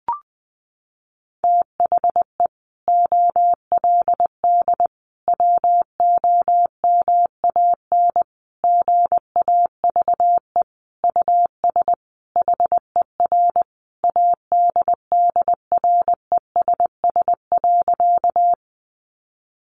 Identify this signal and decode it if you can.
Morse code, heard as a keyed tone, about 20 wpm, 700 Hz. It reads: THE OLD WOMAN GAVE US HER ADDRESS.
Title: Sentences from Top 1100 Words 20wpm